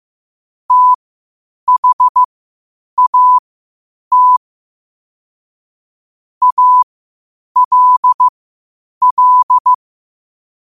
Morse code, heard as a keyed tone, about 15 wpm, spaced as for 8 wpm, 1000 Hz. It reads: THAT ALL
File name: qotdfiles / 8wpm